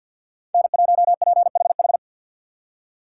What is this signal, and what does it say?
Morse code, heard as a keyed tone, about 64 wpm, 700 Hz. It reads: N1PHH